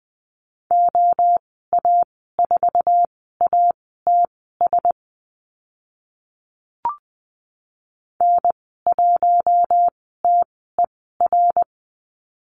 Morse code, heard as a keyed tone, about 20 wpm, spaced as for 14 wpm, 700 Hz. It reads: OA4ATS N1TER
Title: 055 Lesson 10g - Letter r - Call Signs 20wpm